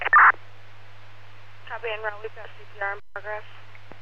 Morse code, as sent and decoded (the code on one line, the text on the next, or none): none